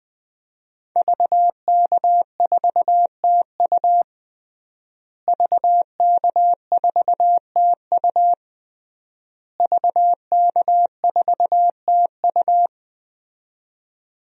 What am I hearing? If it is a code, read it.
Morse code, heard as a keyed tone, about 21 wpm, 700 Hz. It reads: VK4TU VK4TU VK4TU
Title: DXCC-100 Call Signs - 5 Characters - Repeated 3x - 20wpm